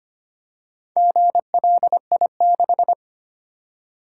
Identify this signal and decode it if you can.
Morse code, heard as a keyed tone, about 25 wpm, 700 Hz. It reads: GLI6
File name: practice / 25wpm / Four Letter-Number Combinations - 25wpm